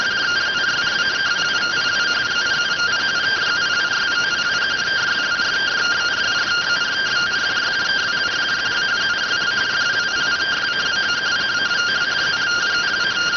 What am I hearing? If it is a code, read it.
Two-tone teleprinter signal, about 170 Hz shift.